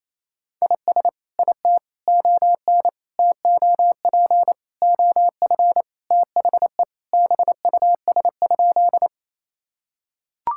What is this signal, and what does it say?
Morse code, heard as a keyed tone, about 27 wpm, 700 Hz. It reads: IS IT ON TOP OF THE BUS?
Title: Sentences from Top 1000 Words - Review Part 6 - 28wpm